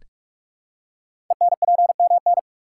Morse code, heard as a keyed tone, about 44 wpm, 700 Hz. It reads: ENPMN
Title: Code Groups - 5 Letter:Numbers - 45wpm